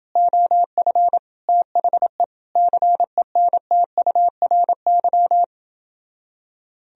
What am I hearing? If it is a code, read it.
Morse code, heard as a keyed tone, about 28 wpm, 700 Hz. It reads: OF THE CENTURY